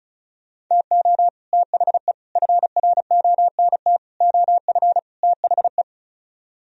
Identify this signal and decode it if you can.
Morse code, heard as a keyed tone, about 46 wpm, 700 Hz. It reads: TO THE FRONT OF THE